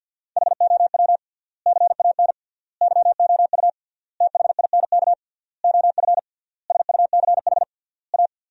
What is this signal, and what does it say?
Morse code, heard as a keyed tone, about 53 wpm, 700 Hz. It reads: HOW CAN YOU THINK OF SUCH A
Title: 3x Word-Spacing - Sentences from Top 200 Words - 50wpm